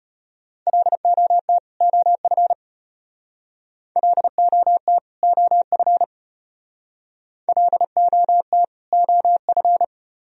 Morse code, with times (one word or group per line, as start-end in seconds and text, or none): 0.7-1.6 LOT
1.8-2.5 OF
4.0-5.0 LOT
5.2-6.0 OF
7.5-8.6 LOT
8.9-9.8 OF